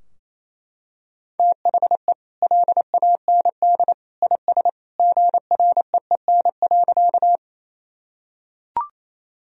Morse code, heard as a keyed tone, about 28 wpm, 700 Hz. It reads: THE LAND IS GREEN.
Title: Sentences from Top 1000 Words - Review Part 2 - 28wpm